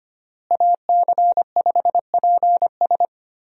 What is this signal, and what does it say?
Morse code, heard as a keyed tone, about 25 wpm, 700 Hz. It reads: AC5PS
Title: USA Call Signs - 5 Characters - Repeated 3x - 25wpm